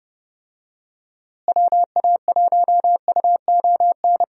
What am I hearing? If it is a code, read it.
Morse code, heard as a keyed tone, about 30 wpm, 700 Hz. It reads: WA1UON
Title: USA Call Signs - 6 Characters - Repeated 2x - 30wpm